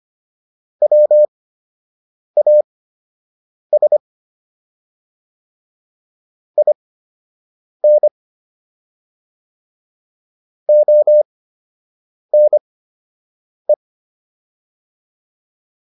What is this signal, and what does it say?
Morse code, heard as a keyed tone, about 25 wpm, spaced as for 7 wpm, 600 Hz. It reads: WAS IN ONE